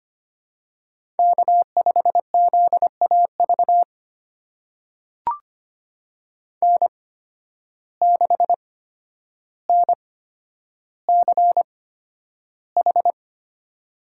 Morse code, heard as a keyed tone, about 25 wpm, spaced as for 11 wpm, 700 Hz. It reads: K5ZAV N6NCH
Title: USA Call Signs - 5 Characters - 8x Character Spacing - 25wpm